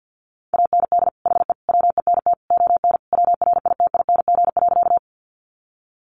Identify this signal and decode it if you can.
Morse code, heard as a keyed tone, about 50 wpm, 700 Hz. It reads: AND HE WENT ON WRITING.
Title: Sentences from Top 800 Words 50wpm